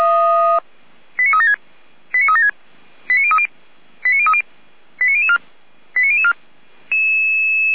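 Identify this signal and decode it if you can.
Five-tone selective call (ZVEI), heard as quick runs of five tones, tones of about 80 ms. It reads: 89376 89376 89030 89030 89004 89004
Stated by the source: ZVEI five-tone call 89001 received on an SDR.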